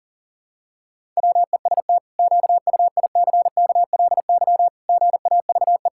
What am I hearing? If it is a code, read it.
Morse code, heard as a keyed tone, about 37 wpm, 700 Hz. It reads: WEST QUICKLY GAVE
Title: Pangram Challenge v2 - 40wpm